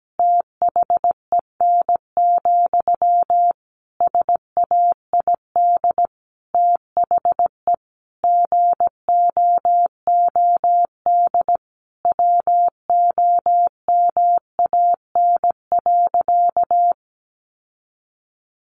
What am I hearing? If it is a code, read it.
Morse code, heard as a keyed tone, about 17 wpm, 700 Hz. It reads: THEN, SAID THE GOOD WOMAN.
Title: Sentences from Top 1100 Words 17wpm